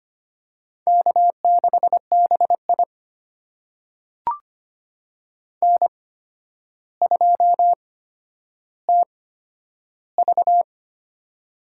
Morse code, heard as a keyed tone, about 25 wpm, spaced as for 11 wpm, 700 Hz. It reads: K6BI N2TV
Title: USA Call Signs - 4 Characters - 8x Character Spacing - 25wpm